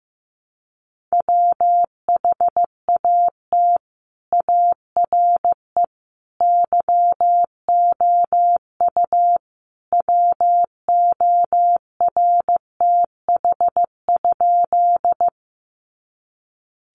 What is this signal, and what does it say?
Morse code, heard as a keyed tone, about 15 wpm, 700 Hz. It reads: WHAT ARE YOU WORTH?
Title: Sentences from Top 800 Words 15wpm